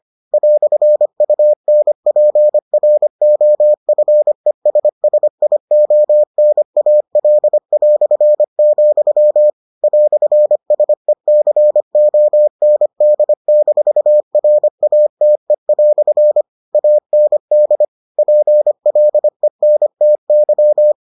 Morse code, heard as a keyed tone, about 25 wpm, 600 Hz. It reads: "UNPROFESSIONAL", "SECOND-RATE" AND PLENTY